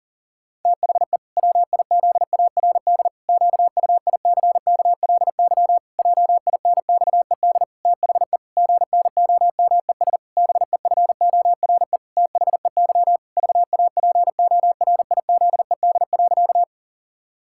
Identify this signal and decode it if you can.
Morse code, heard as a keyed tone, about 40 wpm, 700 Hz. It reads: THE WIZARD QUICKLY JINXED THE GNOMES BEFORE THEY VAPORIZED.
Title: Pangram Challenge v2 - 40wpm